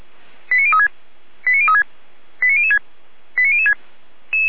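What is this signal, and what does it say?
Five-tone selective call (ZVEI), heard as quick runs of five tones, tones of about 80 ms. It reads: 89036 89036 89006 89006
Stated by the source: ZVEI five-tone selective call received on an SDR.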